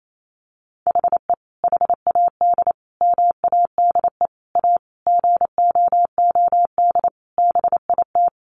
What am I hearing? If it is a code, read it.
Morse code, heard as a keyed tone, about 28 wpm, 700 Hz. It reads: HE HAD MADE A GOOD BIT
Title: Sentences from Top 900 Words 28wpm